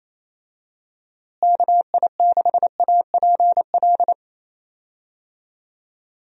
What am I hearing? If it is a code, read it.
Morse code, heard as a keyed tone, about 27 wpm, 700 Hz. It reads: KI6APL
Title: USA Call Signs - 6 Characters - Repeated 3x - 28wpm